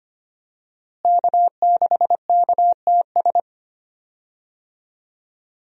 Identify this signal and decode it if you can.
Morse code, heard as a keyed tone, about 25 wpm, 700 Hz. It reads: K6KTS
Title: USA Call Signs - 5 Characters - Repeated 2x - 25wpm